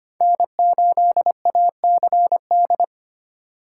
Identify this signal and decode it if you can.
Morse code, heard as a keyed tone, about 25 wpm, 700 Hz. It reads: N8ACD
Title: USA Call Signs - 3x Character Spacing - 25wpm